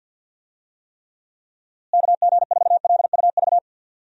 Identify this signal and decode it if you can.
Morse code, heard as a keyed tone, about 53 wpm, 700 Hz. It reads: KG4LUV
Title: USA Call Signs - 6 Characters - Repeated 3x - 50wpm